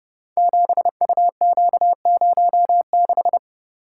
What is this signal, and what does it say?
Morse code, heard as a keyed tone, about 30 wpm, 700 Hz. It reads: 7UQ06
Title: Code Groups - 5 Letter:Numbers - 30wpm